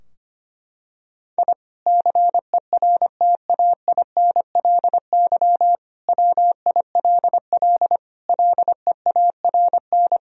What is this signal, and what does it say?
Morse code, heard as a keyed tone, about 25 wpm, 700 Hz. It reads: I CERTAINLY WILL LEARN
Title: Sentences from Top 1000 Words - Review Part 4 - 25wpm